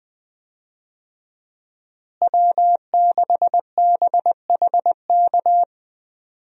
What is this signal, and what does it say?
Morse code, heard as a keyed tone, about 20 wpm, 700 Hz. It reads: W6BHK
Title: USA Call Signs - 5 Characters - Repeated 2x - 20wpm